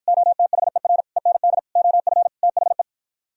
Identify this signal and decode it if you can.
Morse code, heard as a keyed tone, about 54 wpm, 700 Hz. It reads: OTHER END OF THE